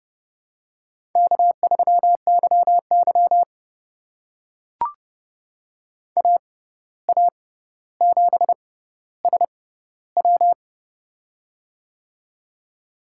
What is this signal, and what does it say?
Morse code, heard as a keyed tone, about 28 wpm, spaced as for 14 wpm, 700 Hz. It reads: K3YY AA7SW
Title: USA Call Signs - 6x Character Spacing - 30wpm